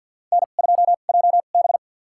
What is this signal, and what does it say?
Morse code, heard as a keyed tone, about 50 wpm, spaced as for 36 wpm, 700 Hz. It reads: N2JB
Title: USA Call Signs - 2x Character Spacing - 50wpm